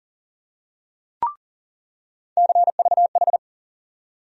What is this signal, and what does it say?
Morse code, heard as a keyed tone, about 37 wpm, 700 Hz. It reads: CVH
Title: USA Call Sign Postfixes - 3 Letters 40wpm